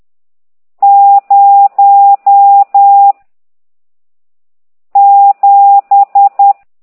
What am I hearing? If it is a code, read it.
Morse code, heard as a keyed tone, about 10 wpm, 800 Hz. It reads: 0 7
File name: ct2idw / exerc14